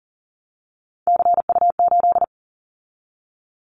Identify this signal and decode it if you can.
Morse code, heard as a keyed tone, about 39 wpm, 700 Hz. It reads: CU8
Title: Callsign Prefixes for Top 100 DXCC - 40wpm